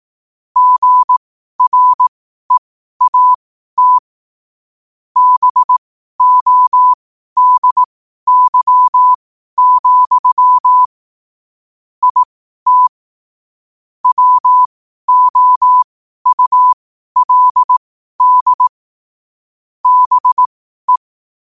Morse code, heard as a keyed tone, about 18 wpm, spaced as for 12 wpm, 1000 Hz. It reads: GREAT BODY, IT WOULD BE
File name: qotdfiles / 12wpm